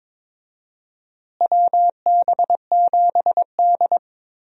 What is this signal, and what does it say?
Morse code, heard as a keyed tone, about 21 wpm, 700 Hz. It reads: WB7D
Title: USA Call Signs - 4 Characters - Repeated 3x - 22wpm